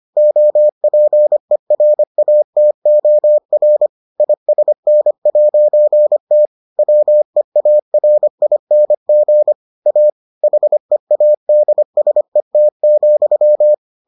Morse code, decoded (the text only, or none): OPERATOR ISN'T WEARING A HEADSET,